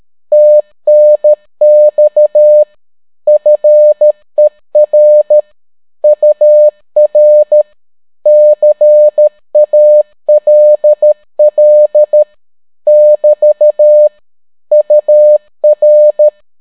Morse code, heard as a keyed tone, about 13 wpm, 600 Hz. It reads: TNX FER UR CALL <BT> UR